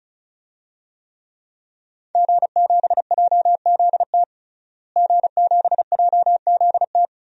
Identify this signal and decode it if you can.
Morse code, heard as a keyed tone, about 34 wpm, 700 Hz. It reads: G7JZT G7JZT
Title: DXCC-100 Call Signs - 5 Characters - Repeated 2x - 35wpm